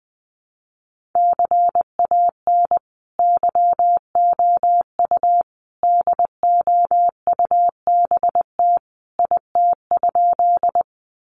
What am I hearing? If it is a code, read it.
Morse code, heard as a keyed tone, about 19 wpm, 700 Hz. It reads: CAN YOU DOUBT IT?
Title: Sentences from Top 1500 Words 20wpm